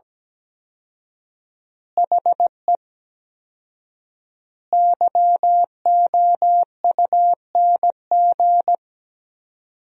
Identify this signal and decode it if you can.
Morse code, heard as a keyed tone, about 17 wpm, 700 Hz. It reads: HE YOUNG
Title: 4x Word-Spacing - Sentences from Top 400 Words - 17wpm